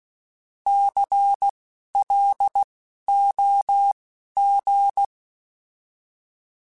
Morse code, heard as a keyed tone, about 16 wpm, spaced as for 11 wpm, 800 Hz. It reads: CLOG
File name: Alphabetical / Alphabetical mp3s / Cs @10